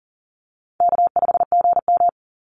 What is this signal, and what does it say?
Morse code, heard as a keyed tone, about 40 wpm, 700 Hz. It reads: K5GM
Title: Easy Call Signs - 40wpm